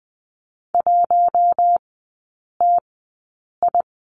Morse code, heard as a keyed tone, about 20 wpm, spaced as for 8 wpm, 700 Hz. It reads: 1TI
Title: 047 Lesson 9e - Number 4 - Three Characters 20wpm